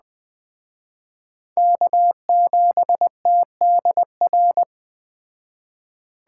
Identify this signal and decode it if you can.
Morse code, heard as a keyed tone, about 20 wpm, 700 Hz. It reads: K7TDR
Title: Speed Racing - USA 1x3 Call Signs 17wpm